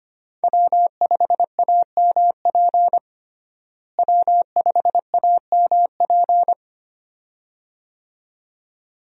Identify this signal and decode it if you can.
Morse code, heard as a keyed tone, about 25 wpm, 700 Hz. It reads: W5AMP W5AMP
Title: USA Call Signs - 5 Characters - Repeated 2x - 25wpm